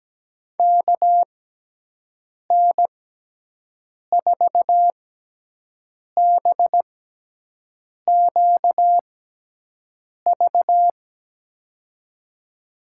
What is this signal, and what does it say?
Morse code, heard as a keyed tone, about 17 wpm, spaced as for 6 wpm, 700 Hz. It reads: KN4BQV
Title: USA Call Signs - 6 Characters - 6x Character Spacing - 17wpm